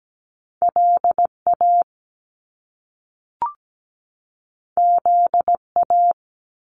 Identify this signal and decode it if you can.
Morse code, heard as a keyed tone, about 17 wpm, 700 Hz. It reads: LA ZA
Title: Top Two Letter Combinations in Words 17wpm